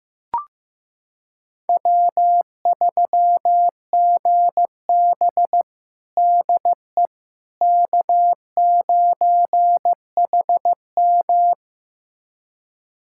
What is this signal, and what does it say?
Morse code, heard as a keyed tone, about 15 wpm, 700 Hz. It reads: W3GB DE K9HM
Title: USA Callsign DE Callsign - 4 Characters - 15wpm